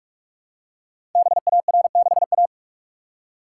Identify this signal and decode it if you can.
Morse code, heard as a keyed tone, about 44 wpm, 700 Hz. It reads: BAR6A
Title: Code Groups - 5 Letter:Numbers - 45wpm